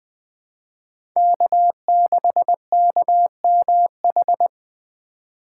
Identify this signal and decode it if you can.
Morse code, heard as a keyed tone, about 20 wpm, 700 Hz. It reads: K6KMH
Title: 2023 SOTA Chaser Call Signs - 20wpm